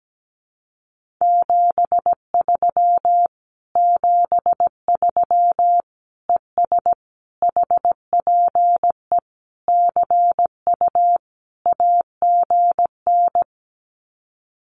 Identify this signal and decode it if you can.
Morse code, heard as a keyed tone, about 17 wpm, 700 Hz. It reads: 73 73 ES HPE CU AGN
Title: Common QSO Phrases Encore 17wpm